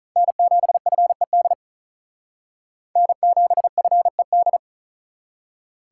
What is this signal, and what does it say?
Morse code, heard as a keyed tone, about 45 wpm, 700 Hz. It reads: N7FED N7FED